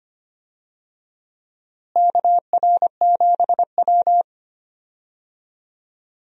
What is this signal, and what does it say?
Morse code, heard as a keyed tone, about 25 wpm, 700 Hz. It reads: KR7W
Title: USA Call Signs - 4 Characters - Repeated 3x - 25wpm